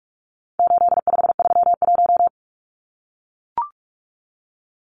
Morse code, heard as a keyed tone, about 44 wpm, 700 Hz. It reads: 8531
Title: Numbers - 4-Digit 45wpm